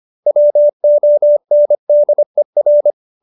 Morse code, heard as a keyed tone, about 25 wpm, 590 Hz. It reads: WONDER